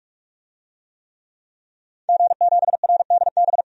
Morse code, heard as a keyed tone, about 46 wpm, 700 Hz. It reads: G7RDB
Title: DXCC-100 Call Signs - 5 Characters - 45wpm